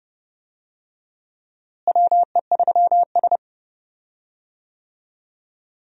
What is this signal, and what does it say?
Morse code, heard as a keyed tone, about 30 wpm, 700 Hz. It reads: WE3S
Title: USA Call Signs - 4 Characters - Repeated 2x - 30wpm